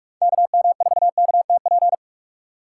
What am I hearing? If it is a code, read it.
Morse code, heard as a keyed tone, about 43 wpm, 700 Hz. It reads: KM4KTP